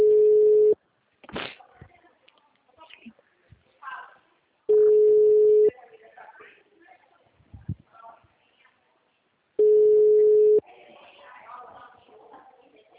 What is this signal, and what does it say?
Ringback tone (Europe, 425 Hz), with speech.
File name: out- -2005- - - .51904